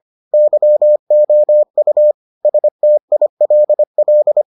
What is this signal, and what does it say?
Morse code, heard as a keyed tone, about 25 wpm, 600 Hz. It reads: YOU STILL